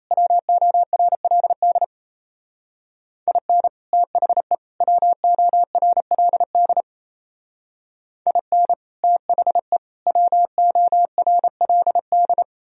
Morse code, heard as a keyed tone, about 34 wpm, 700 Hz. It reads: WORLD IN THE WORLD IN THE WORLD
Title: Speed-Racing - Sets of 3 Words 28wpm